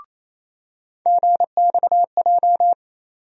Morse code, heard as a keyed tone, about 27 wpm, 700 Hz. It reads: GXJ
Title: USA Call Sign Postfixes - 3 Letters 28wpm